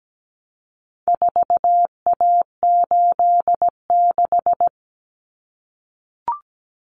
Morse code, heard as a keyed tone, about 17 wpm, 700 Hz. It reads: 4A86
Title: Common Cut Numbers - 4-Digit 17wpm